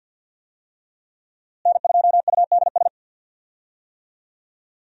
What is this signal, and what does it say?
Morse code, heard as a keyed tone, about 46 wpm, 700 Hz. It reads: N2UDS
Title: USA Call Signs - 5 Characters - Repeated 2x - 50wpm